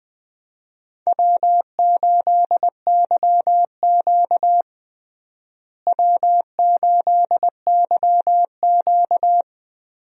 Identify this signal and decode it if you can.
Morse code, heard as a keyed tone, about 20 wpm, 700 Hz. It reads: W8YQ W8YQ
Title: USA Call Signs - 4 Characters - Repeated 3x - 20wpm